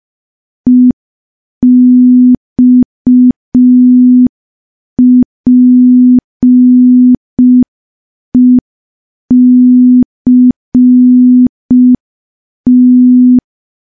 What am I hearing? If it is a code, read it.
Morse code, heard as a keyed tone, about 5 wpm, 250 Hz. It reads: EXPECT
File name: morse / morse-training-05WPM